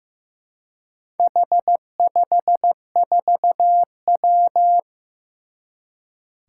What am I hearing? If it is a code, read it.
Morse code, heard as a keyed tone, about 15 wpm, 700 Hz. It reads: H54W
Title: Four Challenging Letter-Number Combinations 15wpm